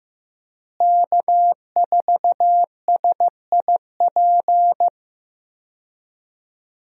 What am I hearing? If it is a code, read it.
Morse code, heard as a keyed tone, about 16 wpm, 700 Hz. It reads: K4SIP
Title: USA Call Signs - 5 Characters - Repeated 2x - 15wpm